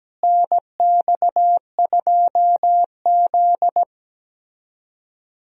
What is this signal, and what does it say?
Morse code, heard as a keyed tone, about 17 wpm, 700 Hz. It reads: NX2Z
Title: USA Call Signs - 4 Characters - Repeated 3x - 17wpm